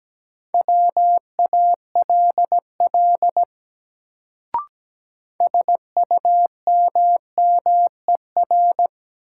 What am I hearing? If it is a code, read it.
Morse code, heard as a keyed tone, about 17 wpm, 700 Hz. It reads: WALL SUMMER
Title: Sentences from Top 1000 Words - Review Part 6 - New Word Drill - 17wpm